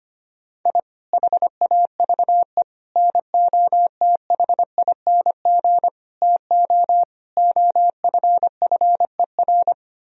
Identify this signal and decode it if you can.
Morse code, heard as a keyed tone, about 25 wpm, 700 Hz. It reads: I HAVE NOTHING TO OFFER
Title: Sentences from Top 1000 Words - Review Part 4 - 25wpm